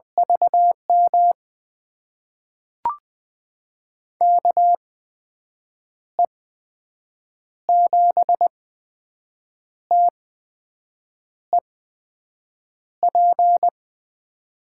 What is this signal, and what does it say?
Morse code, heard as a keyed tone, about 20 wpm, spaced as for 6 wpm, 700 Hz. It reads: VM KE7TEP
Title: USA Call Signs - 6 Characters - 8x Character Spacing - 20wpm